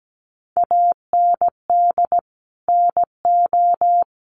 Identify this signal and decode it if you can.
Morse code, heard as a keyed tone, about 17 wpm, 700 Hz. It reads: AND NO
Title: Sentences from Top 1100 Words 17wpm